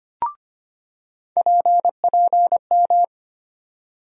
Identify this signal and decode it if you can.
Morse code, heard as a keyed tone, about 25 wpm, 700 Hz. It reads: PPM
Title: USA Call Sign Postfixes - 3 Letters 25wpm